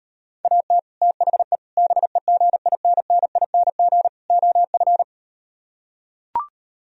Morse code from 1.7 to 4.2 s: BEGINNING

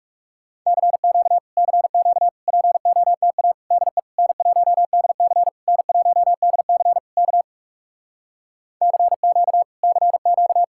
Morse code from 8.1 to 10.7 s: CQ CQ

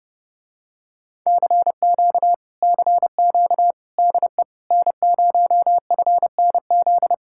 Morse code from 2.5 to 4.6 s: CQ DE

